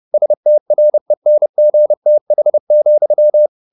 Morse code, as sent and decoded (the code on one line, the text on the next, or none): ... - .-. -. --. - .... --..--
STRENGTH,